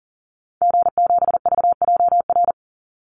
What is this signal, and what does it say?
Morse code, heard as a keyed tone, about 40 wpm, 700 Hz. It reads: G7VJR
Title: Call Signs Repeated 2x - 40wpm